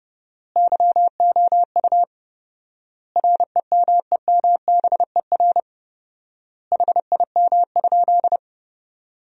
Morse code, heard as a keyed tone, about 32 wpm, 700 Hz. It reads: YOU REMEMBER HIM?
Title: 4x Word-Spacing - Sentences from Top 400 Words - 30wpm